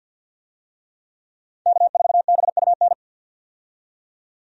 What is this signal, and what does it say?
Morse code, heard as a keyed tone, about 52 wpm, 700 Hz. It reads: K4BUN
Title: USA Call Signs - 5 Characters - 50wpm